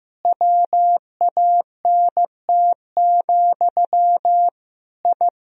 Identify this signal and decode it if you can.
Morse code, heard as a keyed tone, about 15 wpm, 700 Hz. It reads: WANT, I
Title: Sentences from Top 1000 Words - Review Part 3 - 15wpm